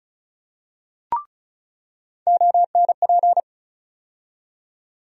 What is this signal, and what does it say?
Morse code, heard as a keyed tone, about 36 wpm, 700 Hz. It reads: ONP
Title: USA Call Sign Postfixes - 3 Letters 35wpm